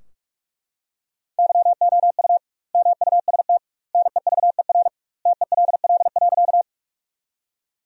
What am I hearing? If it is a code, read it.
Morse code, heard as a keyed tone, about 42 wpm, 700 Hz. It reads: YOU MUST NEVER TELL.
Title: 2x Word-Spacing - Sentences from Top 300 Words - 45wpm